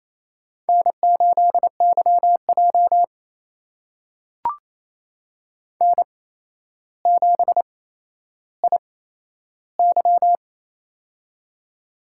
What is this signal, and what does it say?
Morse code, heard as a keyed tone, about 28 wpm, spaced as for 12 wpm, 700 Hz. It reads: N8YJ N7IY
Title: USA Call Signs - 4 Characters - 8x Character Spacing - 28wpm